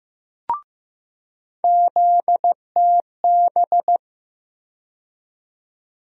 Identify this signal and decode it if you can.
Morse code, heard as a keyed tone, about 15 wpm, 700 Hz. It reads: ZTB
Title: USA Call Sign Postfixes - 3 Letters 15wpm